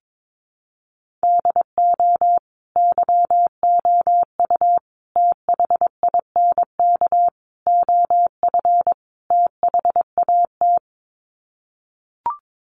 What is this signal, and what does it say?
Morse code, heard as a keyed tone, about 22 wpm, 700 Hz. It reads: DO YOU THINK OF THAT
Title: Sets of 5 Words Encore2 22wpm